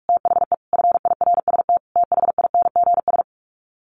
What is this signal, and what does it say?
Morse code, heard as a keyed tone, about 44 wpm, 700 Hz. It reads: THE FIRST THINGS
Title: Sets of 5 Words Encore 45wpm